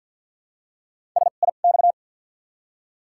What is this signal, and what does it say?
Morse code, heard as a keyed tone, about 48 wpm, spaced as for 35 wpm, 700 Hz. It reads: SIX
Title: 2x Character Spacing - Top 500 Words - 50wpm